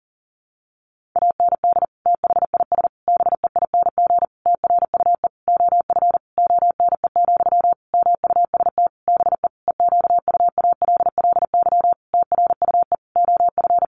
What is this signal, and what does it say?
Morse code, heard as a keyed tone, about 40 wpm, 700 Hz. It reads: AND THIS BEING TRUE OF ONE, MUST BE EQUALLY TRUE OF
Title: Sentences from Top 1400 Words 40wpm